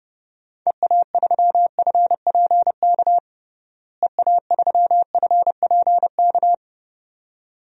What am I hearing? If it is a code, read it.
Morse code, heard as a keyed tone, about 30 wpm, 700 Hz. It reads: EA3FPK EA3FPK
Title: DXCC-100 Call Signs - 6 Characters - Repeated 2x - 30wpm